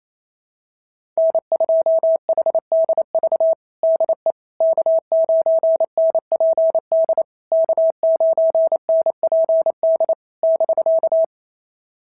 Morse code, heard as a keyed tone, about 28 wpm, 650 Hz. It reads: N2HDV DE K9NPD K9NPD <BK>
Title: SOTA QSO Practice 28wpm